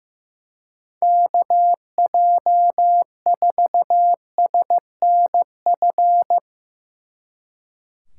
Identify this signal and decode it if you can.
Morse code, heard as a keyed tone, about 16 wpm, 700 Hz. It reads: KJ4SNF
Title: Speed Racing - USA 2x3 Call Signs 15wpm